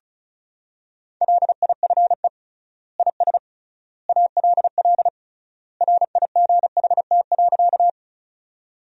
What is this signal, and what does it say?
Morse code, heard as a keyed tone, about 36 wpm, 700 Hz. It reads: LIFE IS ALL RIGHT.
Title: 3x Word-Spacing - Sentences from Top 300 Words - 35wpm